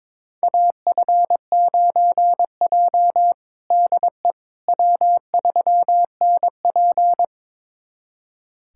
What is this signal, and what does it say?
Morse code, heard as a keyed tone, about 22 wpm, 700 Hz. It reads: AF9J DE W3NP
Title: USA Callsign DE Callsign - 4 Characters - 22wpm